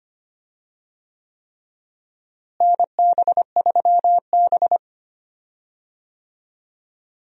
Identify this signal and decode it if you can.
Morse code, heard as a keyed tone, about 25 wpm, 700 Hz. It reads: NB3B
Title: USA Call Signs - 4 Characters - Repeated 3x - 25wpm